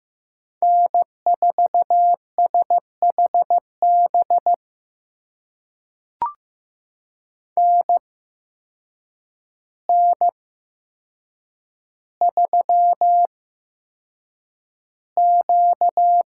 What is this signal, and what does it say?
Morse code, heard as a keyed tone, about 15 wpm, spaced as for 7 wpm, 700 Hz. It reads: N4SHB NN3Q
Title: USA Call Signs - 8x Character Spacing - 15wpm